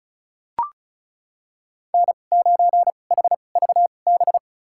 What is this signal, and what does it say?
Morse code, heard as a keyed tone, about 34 wpm, spaced as for 25 wpm, 700 Hz. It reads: N9HVB
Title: USA Call Signs - 5 Characters - 2x Character Spacing - 35wpm